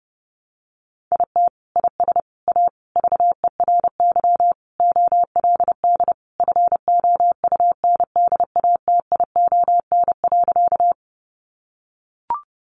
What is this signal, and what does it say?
Morse code, heard as a keyed tone, about 30 wpm, 700 Hz. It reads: IT IS A VERY OLD FOUNDATION.
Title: Sentences from Top 1200 Words 30wpm